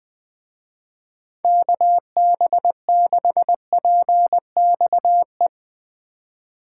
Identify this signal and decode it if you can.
Morse code, heard as a keyed tone, about 22 wpm, 700 Hz. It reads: KB6PXE
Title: Speed Racing - USA Call Signs 20wpm